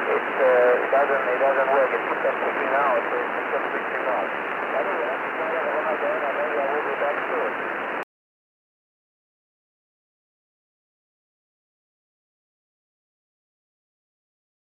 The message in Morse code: none